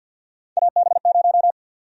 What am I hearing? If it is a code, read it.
Morse code, heard as a keyed tone, about 51 wpm, 700 Hz. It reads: AB0
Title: USA Call Sign Prefixes - 50wpm